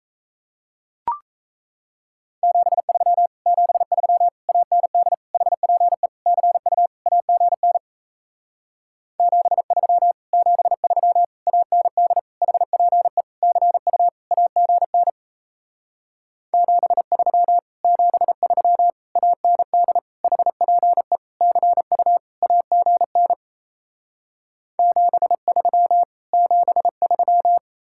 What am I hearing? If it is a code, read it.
Morse code, heard as a keyed tone, about 38 wpm, 700 Hz. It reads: 73 73 AND HPE CU AGN 73 73 AND HPE CU AGN 73 73 AND HPE CU AGN 73 73